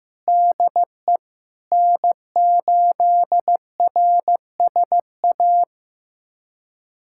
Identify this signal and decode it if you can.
Morse code, heard as a keyed tone, about 15 wpm, 700 Hz. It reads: DE N8RSA
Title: USA Callsign DE Callsign - 5 Characters - 15wpm